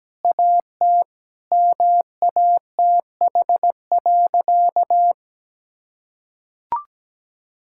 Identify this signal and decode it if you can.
Morse code, heard as a keyed tone, about 17 wpm, 700 Hz. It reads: AT MATH.
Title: Sentences from Top 1000 Words - Review Part 6 - 17wpm